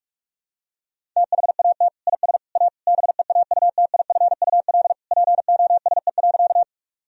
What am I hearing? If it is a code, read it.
Morse code, heard as a keyed tone, about 47 wpm, 700 Hz. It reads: THAT IS A BEAUTIFUL POSE.
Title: Sentences from Top 1000 Words - Review Part 2 - 45wpm